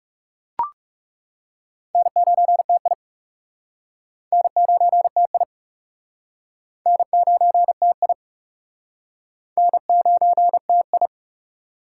Morse code, heard as a keyed tone, about 40 wpm, 700 Hz. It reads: N9TI N9TI N9TI N9TI